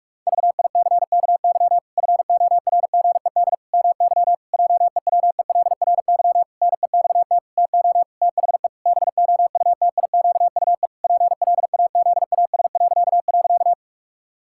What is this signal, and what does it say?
Morse code, heard as a keyed tone, about 46 wpm, 700 Hz. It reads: VICKY FORGED MY JEWELRY NEXT TO THE BOUTIQUE PLAZAS..
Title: Pangram Challenge v2 - 45wpm